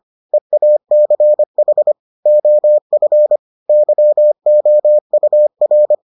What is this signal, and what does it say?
Morse code, heard as a keyed tone, about 25 wpm, 600 Hz. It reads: EACH OF YOUR